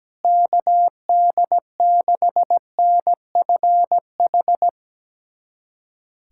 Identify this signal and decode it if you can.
Morse code, heard as a keyed tone, about 17 wpm, 700 Hz. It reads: KD6NFH
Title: USA Callsign DE Callsign - 6 Characters - 17wpm